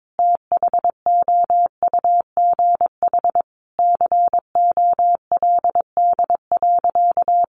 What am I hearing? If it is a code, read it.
Morse code, heard as a keyed tone, about 22 wpm, 700 Hz. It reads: THOUGH COLD.
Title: Sentences from Top 1000 Words 22wpm